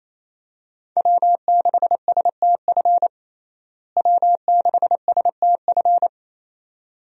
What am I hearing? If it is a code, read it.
Morse code, heard as a keyed tone, about 28 wpm, 700 Hz. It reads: W6STF W6STF
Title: USA Call Signs - 5 Characters - Repeated 2x - 28wpm